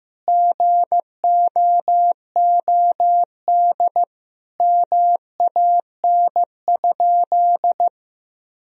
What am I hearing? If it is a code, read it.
Morse code, heard as a keyed tone, about 15 wpm, 700 Hz. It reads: GOOD MAN?
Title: Sentences from Top 1000 Words - Review Part 4 - 15wpm